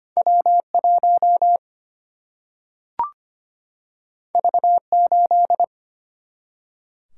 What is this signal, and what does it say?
Morse code, heard as a keyed tone, about 25 wpm, 700 Hz. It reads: W 1 V 8
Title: Single-Letter Single-Number 25wpm